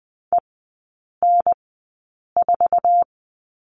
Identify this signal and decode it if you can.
Morse code, heard as a keyed tone, about 20 wpm, spaced as for 8 wpm, 700 Hz. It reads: EN4
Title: 047 Lesson 9e - Number 4 - Three Characters 20wpm